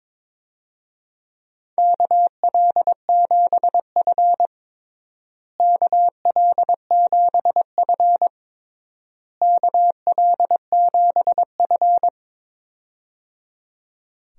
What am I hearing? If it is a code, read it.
Morse code, heard as a keyed tone, about 23 wpm, 700 Hz. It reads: KL7F KL7F KL7F
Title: DXCC-100 Call Signs - 4 Characters - Repeated 3x - 22wpm